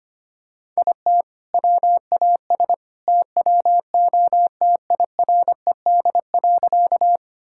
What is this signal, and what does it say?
Morse code, heard as a keyed tone, about 25 wpm, 700 Hz. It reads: IT WAS TWO TIRED.
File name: practice / 25wpm / All-Age Appropriate Jokes - 25wpm